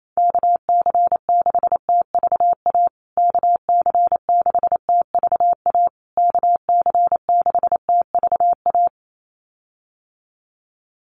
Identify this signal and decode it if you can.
Morse code, heard as a keyed tone, about 28 wpm, 700 Hz. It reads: KC6TVA KC6TVA KC6TVA
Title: USA Call Signs Repeated 3x 28wpm